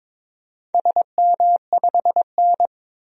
Morse code, heard as a keyed tone, about 22 wpm, 700 Hz. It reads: SM5N
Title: DXCC-100 Call Signs - 4 Characters - 22wpm